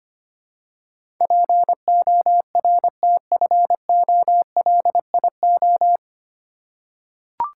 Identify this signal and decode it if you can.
Morse code, heard as a keyed tone, about 25 wpm, 700 Hz. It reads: PORTFOLIO
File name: Nine Letter Words - 25wpm